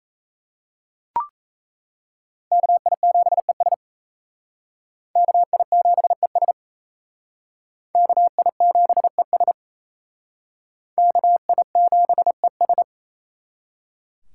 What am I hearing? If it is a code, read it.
Morse code, heard as a keyed tone, about 34 wpm, 700 Hz. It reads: KI7ES KI7ES KI7ES KI7ES
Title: Speed Racing - USA 2x2 Call Signs 28wpm